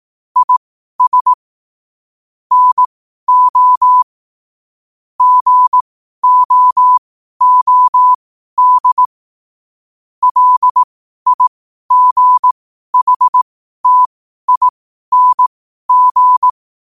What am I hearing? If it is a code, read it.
Morse code, heard as a keyed tone, about 18 wpm, spaced as for 12 wpm, 1000 Hz. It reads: IS NO GOOD LIGHTING
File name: qotdfiles / 12wpm